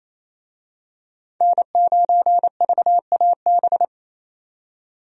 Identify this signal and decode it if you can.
Morse code, heard as a keyed tone, about 28 wpm, 700 Hz. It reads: N9VAB
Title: USA Call Signs - 5 Characters - 28wpm